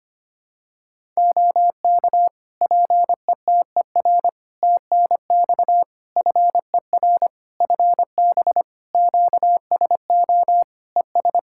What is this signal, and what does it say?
Morse code, heard as a keyed tone, about 25 wpm, 700 Hz. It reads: OK PETER TNX FER FB QSO ES
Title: Practice QSO - Exchange 4 - Ending - 25wpm